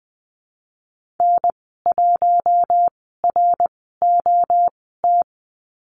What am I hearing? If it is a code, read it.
Morse code, heard as a keyed tone, about 20 wpm, spaced as for 15 wpm, 700 Hz. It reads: N1ROT
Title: 055 Lesson 10g - Letter r - Call Signs 20wpm